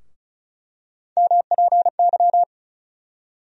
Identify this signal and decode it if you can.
Morse code, heard as a keyed tone, about 36 wpm, 700 Hz. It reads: MPY
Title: USA Call Sign Postfixes - 3 Letters 35wpm